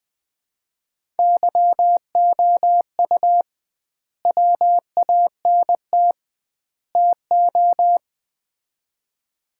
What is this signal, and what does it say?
Morse code, heard as a keyed tone, about 21 wpm, 700 Hz. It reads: YOU WANT TO
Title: 2x Word-Spacing - Sets of 3 Words - 20wpm